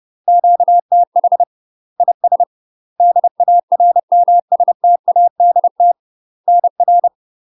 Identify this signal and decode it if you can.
Morse code, heard as a keyed tone, about 30 wpm, 700 Hz. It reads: QTH IS DARMSTADT NR
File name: UK MASTER QSOs TRAINING/30WPM